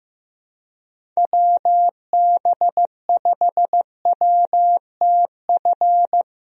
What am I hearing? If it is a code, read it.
Morse code, heard as a keyed tone, about 15 wpm, 700 Hz. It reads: WB5WTF
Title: USA Call Signs - 6 Characters - Repeated 2x - 15wpm